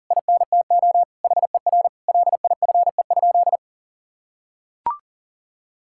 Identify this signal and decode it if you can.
Morse code, heard as a keyed tone, about 40 wpm, 700 Hz. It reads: INTO HER LIFE?
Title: Sentences from Top 1000 Words - Review Part 6 - 40wpm